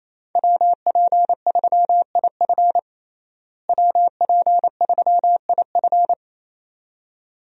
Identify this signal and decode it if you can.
Morse code, heard as a keyed tone, about 27 wpm, 700 Hz. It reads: WP3IF WP3IF
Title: USA Call Signs - 5 Characters - Repeated 2x - 28wpm